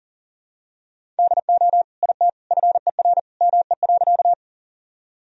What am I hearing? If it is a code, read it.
Morse code, heard as a keyed tone, about 40 wpm, 700 Hz. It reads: DO IT FER ME.
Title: Sentences from Top 200 Words and Abbreviations 40wpm